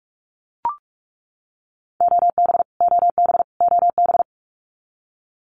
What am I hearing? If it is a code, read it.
Morse code, heard as a keyed tone, about 44 wpm, 700 Hz. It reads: OB OB OB
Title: Abbreviations Repeated 3x - 45wpm